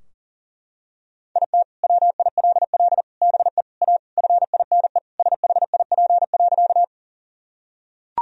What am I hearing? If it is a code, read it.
Morse code, heard as a keyed tone, about 40 wpm, 700 Hz. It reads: IT WILL BE A FINE SHIP.
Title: Sentences from Top 1000 Words - Review Part 2 - 40wpm